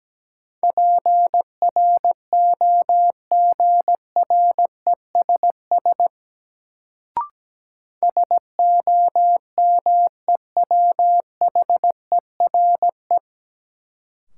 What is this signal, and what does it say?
Morse code, heard as a keyed tone, about 17 wpm, 700 Hz. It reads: PROGRESS SOMEWHERE
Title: Sentences from Top 1400 Words - New Word Drill - 17wpm